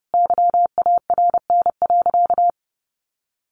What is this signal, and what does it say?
Morse code, heard as a keyed tone, about 30 wpm, 700 Hz. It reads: YARN.
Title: Additional Easy Phrases 30wpm